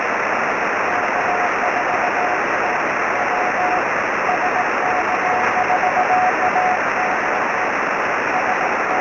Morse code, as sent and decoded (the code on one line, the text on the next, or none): none